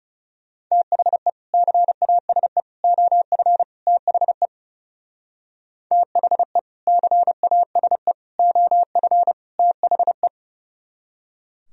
Morse code, heard as a keyed tone, about 38 wpm, 700 Hz. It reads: THE CASE OF THE THE CASE OF THE